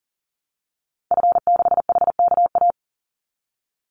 Morse code, heard as a keyed tone, about 40 wpm, 700 Hz. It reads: F6HKA
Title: Call Signs Repeated 2x - 40wpm